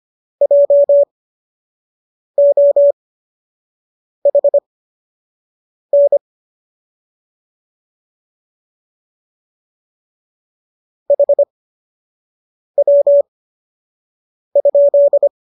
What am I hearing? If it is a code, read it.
Morse code, heard as a keyed tone, about 25 wpm, spaced as for 6 wpm, 570 Hz. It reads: JOHN HW?